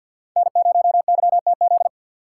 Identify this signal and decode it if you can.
Morse code, heard as a keyed tone, about 62 wpm, 700 Hz. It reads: N0YTZ
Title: Speed Racing - USA Call Signs 50wpm